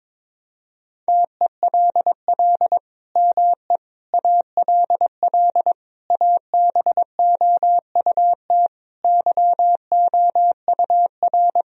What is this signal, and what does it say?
Morse code, heard as a keyed tone, about 22 wpm, 700 Hz. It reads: TELL ME ALL ABOUT YOUR
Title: Sentences from Top 1000 Words - Review Part 4 - 22wpm